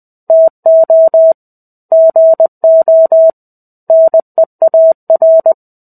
Morse code, heard as a keyed tone, about 20 wpm, 650 Hz. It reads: TO GO NEAR
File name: Chapter-05 20 WPM